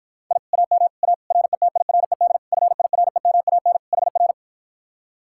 Morse code, heard as a keyed tone, about 54 wpm, 700 Hz. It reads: I AM A RETIRED FIREMAN HR